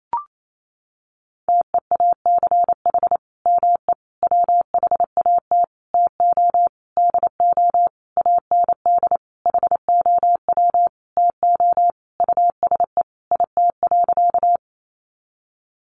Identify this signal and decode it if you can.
Morse code, heard as a keyed tone, about 28 wpm, 700 Hz. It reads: TEACH ME WHAT TO DO AND HOW TO USE IT.
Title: Sentences from Top 500 Words 28wpm